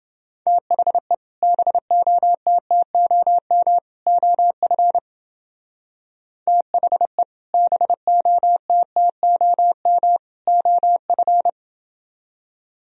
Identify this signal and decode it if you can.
Morse code, heard as a keyed tone, about 29 wpm, 700 Hz. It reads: THE BOTTOM OF THE BOTTOM OF